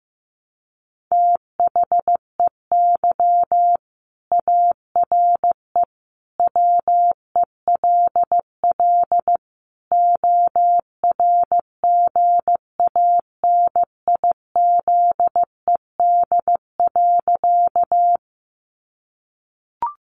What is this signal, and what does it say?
Morse code, heard as a keyed tone, about 15 wpm, 700 Hz. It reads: THEY ARE WELL ORGANIZED.
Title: Sentences from Top 1300 Words 15wpm